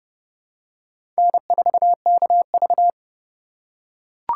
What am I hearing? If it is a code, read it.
Morse code, heard as a keyed tone, about 30 wpm, 700 Hz. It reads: N4KV
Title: USA Call Signs - 4 Characters - 3x Character Spacing - 30wpm